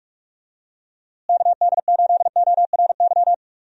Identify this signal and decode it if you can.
Morse code, heard as a keyed tone, about 42 wpm, 700 Hz. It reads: KD8ORY
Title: USA Call Signs - 6 Characters - Repeated 2x - 45wpm